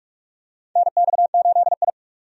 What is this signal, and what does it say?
Morse code, heard as a keyed tone, about 44 wpm, 700 Hz. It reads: NX8I
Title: USA Call Signs - 4 Characters - 45wpm